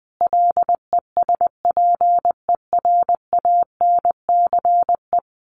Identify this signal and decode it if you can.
Morse code, heard as a keyed tone, about 20 wpm, 700 Hz. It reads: LESPERANCE
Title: A Sound Of Thunder 20wpm